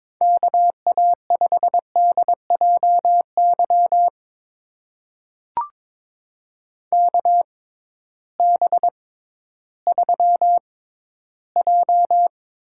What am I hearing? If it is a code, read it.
Morse code, heard as a keyed tone, about 22 wpm, spaced as for 13 wpm, 700 Hz. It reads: KA5DJY KB3J
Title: USA Call Signs - 6 Characters - 6x Character Spacing - 22wpm